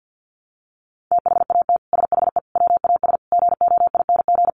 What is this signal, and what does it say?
Morse code, heard as a keyed tone, about 50 wpm, 700 Hz. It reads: THAT SHE WAS GOING
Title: Sets of 5 Words Encore2 50wpm